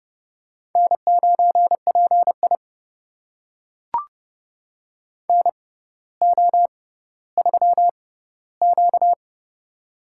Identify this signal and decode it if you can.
Morse code, heard as a keyed tone, about 30 wpm, spaced as for 15 wpm, 700 Hz. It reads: N9PI NO3Q
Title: USA Call Signs - 6x Character Spacing - 30wpm